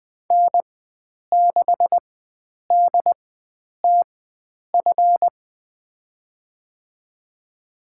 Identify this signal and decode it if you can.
Morse code, heard as a keyed tone, about 20 wpm, spaced as for 9 wpm, 700 Hz. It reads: N6DTF
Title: USA Call Signs - 4x Character Spacing - 20wpm